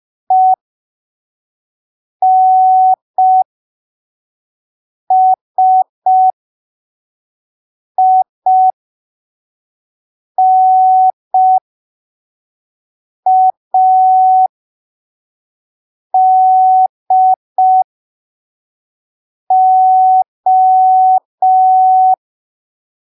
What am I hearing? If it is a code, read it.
Morse code, heard as a keyed tone, about 5 wpm, spaced as for 3 wpm, 750 Hz. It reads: ENSINADO
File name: arquivos / texto / texto 05 5ppm